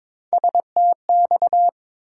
Morse code, heard as a keyed tone, about 22 wpm, 700 Hz. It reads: STX
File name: practice / 22wpm / ARRL Sections v2 - 22wpm